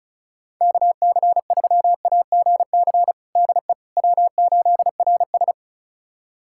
Morse code, heard as a keyed tone, about 36 wpm, 700 Hz. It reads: KC3AGC DE W8RS
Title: USA Callsign DE Callsign 35wpm